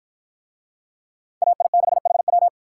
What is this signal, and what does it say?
Morse code, heard as a keyed tone, about 53 wpm, 700 Hz. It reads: AI6HW